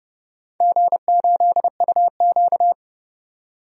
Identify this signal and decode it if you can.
Morse code, heard as a keyed tone, about 29 wpm, 700 Hz. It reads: G8UQ
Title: DXCC-100 Call Signs - 4 Characters - 30wpm